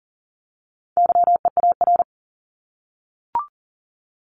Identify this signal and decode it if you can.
Morse code, heard as a keyed tone, about 40 wpm, 700 Hz. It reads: YEAR
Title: ES:S Suffix - Top 100 Words with Base Words 40wpm